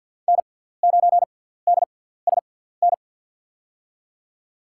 Morse code, heard as a keyed tone, about 47 wpm, spaced as for 17 wpm, 700 Hz. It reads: N9DSN
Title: USA Call Signs - 5 Characters - 6x Character Spacing - 50wpm